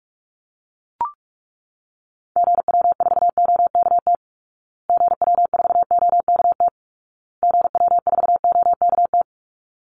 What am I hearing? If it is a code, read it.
Morse code, heard as a keyed tone, about 44 wpm, 700 Hz. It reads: GW4OKT GW4OKT GW4OKT
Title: Call Signs Repeated 2x - 45wpm